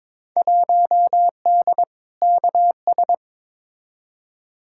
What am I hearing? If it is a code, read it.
Morse code, heard as a keyed tone, about 23 wpm, 700 Hz. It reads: 1D KS
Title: Field Day Exchanges - 22wpm